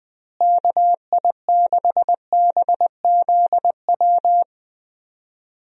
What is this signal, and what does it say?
Morse code, heard as a keyed tone, about 20 wpm, 700 Hz. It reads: KI6BZW
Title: USA Call Signs - 6 Characters - Repeated 3x - 20wpm